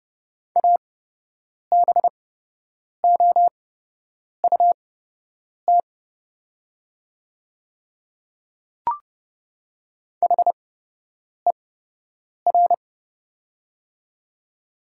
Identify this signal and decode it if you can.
Morse code, heard as a keyed tone, about 30 wpm, spaced as for 8 wpm, 700 Hz. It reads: ABOUT HER
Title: 8x Character Spacing - Top 100 Words - 30wpm